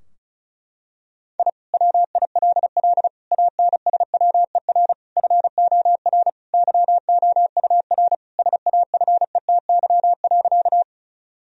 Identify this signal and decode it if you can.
Morse code, heard as a keyed tone, about 34 wpm, 700 Hz. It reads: I WILL ANSWER FOR YOUR SAFETY.
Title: Sentences from Top 1000 Words - Review Part 4 - 35wpm